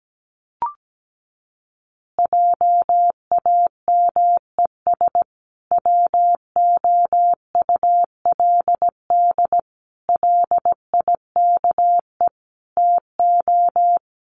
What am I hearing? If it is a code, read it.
Morse code, heard as a keyed tone, about 17 wpm, 700 Hz. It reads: JAMES WOULD LIKE TO
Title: Easy Phrases 17wpm